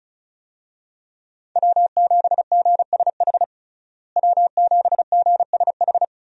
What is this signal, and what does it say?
Morse code, heard as a keyed tone, about 34 wpm, 700 Hz. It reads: W7GSH W7GSH
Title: USA Call Signs - 5 Characters - Repeated 2x - 35wpm